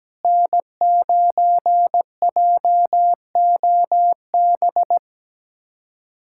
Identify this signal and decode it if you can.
Morse code, heard as a keyed tone, about 17 wpm, 700 Hz. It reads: N9JOB
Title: USA Callsign DE Callsign - 5 Characters - 17wpm